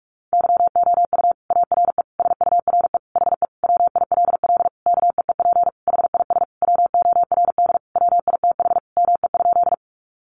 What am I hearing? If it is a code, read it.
Morse code, heard as a keyed tone, about 44 wpm, 700 Hz. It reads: YOU ARE SURE HE WILL KEEP HIS WORD WITH ME?
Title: Sentences from Top 300 Words 45wpm